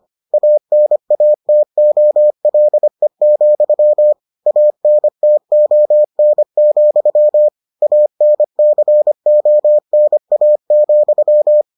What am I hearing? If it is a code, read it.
Morse code, heard as a keyed tone, about 25 wpm, 600 Hz. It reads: ANATOLE, ANTON, ANCONA,